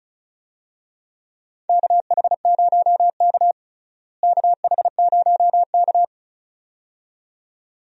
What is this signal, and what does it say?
Morse code, heard as a keyed tone, about 36 wpm, 700 Hz. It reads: KH0K KH0K
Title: USA Call Signs - 4 Characters - Repeated 2x - 35wpm